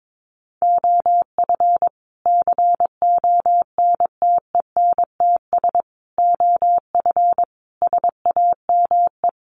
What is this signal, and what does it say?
Morse code, heard as a keyed tone, about 22 wpm, 700 Hz. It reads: OF CONTENTS OF SAME
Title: Sets of 5 Words Encore2 22wpm